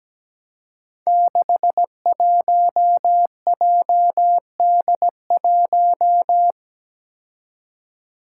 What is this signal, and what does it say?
Morse code, heard as a keyed tone, about 17 wpm, 700 Hz. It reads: 61JD1
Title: Five Challenging Letter-Number Combinations 17wpm